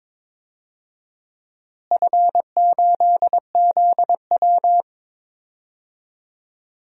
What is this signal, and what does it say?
Morse code, heard as a keyed tone, about 22 wpm, 700 Hz. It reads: F8ZW
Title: DXCC-100 Call Signs - 4 Characters - Repeated 3x - 22wpm